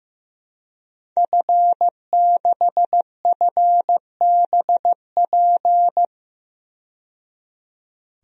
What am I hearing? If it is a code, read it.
Morse code, heard as a keyed tone, about 15 wpm, 700 Hz. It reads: F6FBP
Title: DXCC-100 Call Signs - 5 Characters - 15wpm